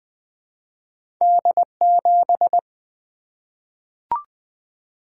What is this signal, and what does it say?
Morse code, heard as a keyed tone, about 20 wpm, 700 Hz. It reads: D7
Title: Two Letter-Number Combinations - 20wpm